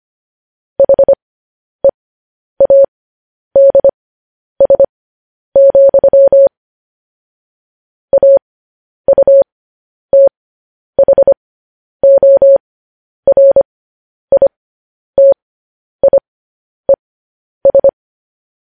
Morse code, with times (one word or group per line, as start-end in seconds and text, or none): 0.8-6.5 HEADS,
8.1-17.9 AUTHORITIES